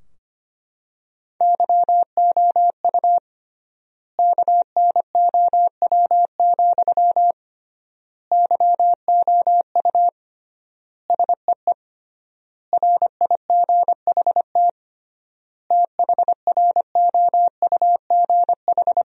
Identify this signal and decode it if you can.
Morse code, heard as a keyed tone, about 25 wpm, 700 Hz. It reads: YOU KNOW, YOU SEE RIGHT THROUGH
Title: 3x Word-Spacing - Sentences from Top 200 Words - 25wpm